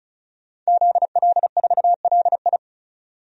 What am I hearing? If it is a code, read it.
Morse code, heard as a keyed tone, about 36 wpm, 700 Hz. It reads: ZL4LI
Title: DXCC-100 Call Signs - 5 Characters - 35wpm